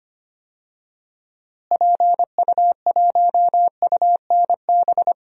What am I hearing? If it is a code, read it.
Morse code, heard as a keyed tone, about 25 wpm, 700 Hz. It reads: PU1UNB
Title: DXCC-100 Call Signs - 6 Characters - Repeated 2x - 25wpm